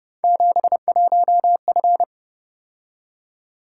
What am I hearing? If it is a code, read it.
Morse code, heard as a keyed tone, about 30 wpm, 700 Hz. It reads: 71F